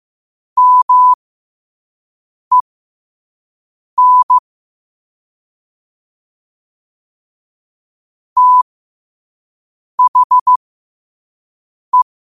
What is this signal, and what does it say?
Morse code, heard as a keyed tone, about 15 wpm, spaced as for 5 wpm, 1000 Hz. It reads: MEN THE
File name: qotdfiles / 5wpm